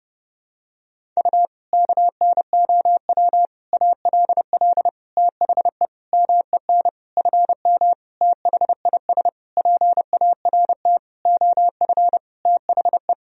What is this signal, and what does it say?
Morse code, heard as a keyed tone, about 30 wpm, 700 Hz. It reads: U KNOW ALL THE MEN FM THIS PART OF THE
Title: Sentences from Top 300 Words and Abbreviations 30wpm